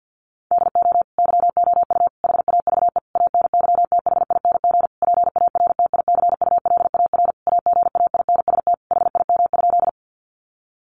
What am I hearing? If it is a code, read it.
Morse code, heard as a keyed tone, about 50 wpm, 700 Hz. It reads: DO YOU HAVE ANYTHING PARTICULAR AGAINST HIM?
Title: Sentences from Top 1000 Words 50wpm